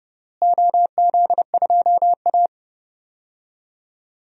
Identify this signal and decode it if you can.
Morse code, heard as a keyed tone, about 30 wpm, 700 Hz. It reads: OZ2A
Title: DXCC-100 Call Signs - 4 Characters - Repeated 3x - 30wpm